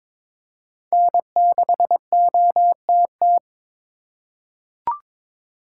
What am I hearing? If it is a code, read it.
Morse code, heard as a keyed tone, about 23 wpm, 700 Hz. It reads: N6OTT
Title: USA Call Signs - 4x Character Spacing - 22wpm